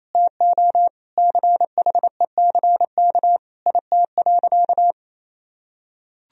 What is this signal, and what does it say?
Morse code, heard as a keyed tone, about 28 wpm, 700 Hz. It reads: TO CHECK IT.
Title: Sentences from Top 1000 Words - Review Part 3 - 28wpm